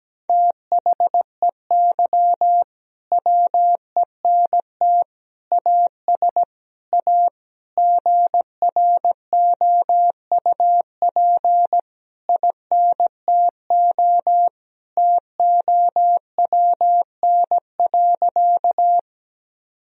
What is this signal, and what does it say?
Morse code, heard as a keyed tone, about 17 wpm, 700 Hz. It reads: THEY WENT AS A GROUP INTO TOWN.
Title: Sentences from Top 1000 Words - Review Part 1 - 17wpm